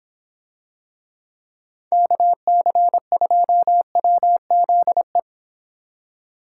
Morse code, heard as a keyed tone, about 25 wpm, 700 Hz. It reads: KC2WZE